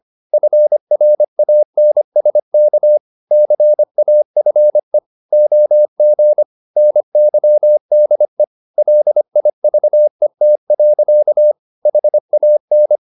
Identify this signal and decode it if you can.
Morse code, heard as a keyed tone, about 25 wpm, 600 Hz. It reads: FRANSK CAFE OG NYDE LIVET. HAN
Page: Ferie 25 wpm